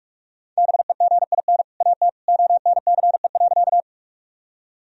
Morse code, heard as a keyed tone, about 46 wpm, 700 Hz. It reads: BEGIN AT ONCE.
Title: Sentences from Top 1000 Words - Review Part 1 - 45wpm